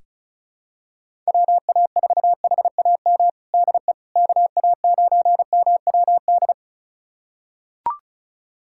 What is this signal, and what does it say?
Morse code, heard as a keyed tone, about 34 wpm, 700 Hz. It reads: WA4HAM DE KA9MWD
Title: USA Callsign DE Callsign - 6 Characters - 35wpm